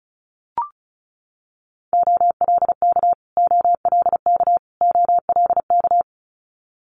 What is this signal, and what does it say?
Morse code, heard as a keyed tone, about 34 wpm, 700 Hz. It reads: OLK OLK OLK
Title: Multi-letter Phonemes Repeated 3x 35wpm